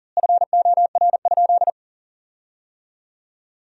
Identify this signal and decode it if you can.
Morse code, heard as a keyed tone, about 40 wpm, 700 Hz. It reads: FOR?
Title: 3x Word-Spacing - Sentences from Top 200 Words - 40wpm